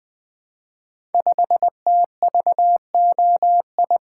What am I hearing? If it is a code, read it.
Morse code, heard as a keyed tone, about 20 wpm, 700 Hz. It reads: 5TVOI
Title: Code Groups - 5 Letter:Numbers - 20wpm